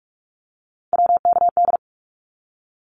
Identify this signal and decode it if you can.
Morse code, heard as a keyed tone, about 44 wpm, 700 Hz. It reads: WKD
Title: Abbreviations 45wpm